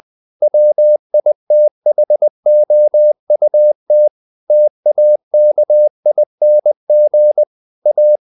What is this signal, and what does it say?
Morse code, heard as a keyed tone, about 20 wpm, 600 Hz. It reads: WITHOUT TAKING A
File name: the old man and the sea / the old man and the sea part 1 20wpm